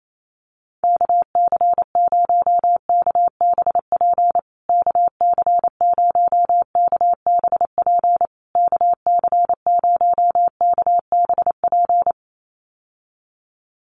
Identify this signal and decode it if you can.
Morse code, heard as a keyed tone, about 28 wpm, 700 Hz. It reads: KC0KBP KC0KBP KC0KBP
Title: USA Call Signs Repeated 3x 28wpm